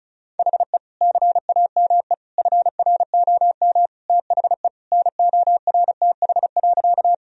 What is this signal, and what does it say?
Morse code, heard as a keyed tone, about 36 wpm, 700 Hz. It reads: HE CAME FROM THE NORTH.
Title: Sentences from Top 1000 Words - Review Part 1 - 35wpm